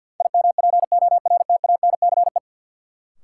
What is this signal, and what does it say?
Morse code, heard as a keyed tone, about 53 wpm, 700 Hz. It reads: IMPORTANCE
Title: Sentences from Top 1200 Words - New Word Drill - 50wpm